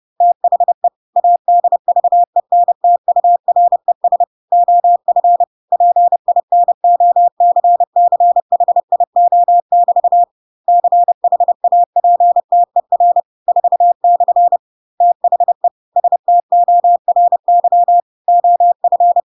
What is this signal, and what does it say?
Morse code, heard as a keyed tone, about 30 wpm, 700 Hz. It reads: THE ADVENTURES OF PINOCCHIO<BT> CHAPTER 4/ THE STORY OF